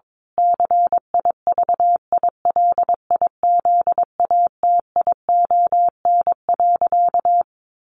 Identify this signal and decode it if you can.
Morse code, heard as a keyed tone, about 22 wpm, 700 Hz. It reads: CIVILIZATION.
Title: A Sound Of Thunder 22wpm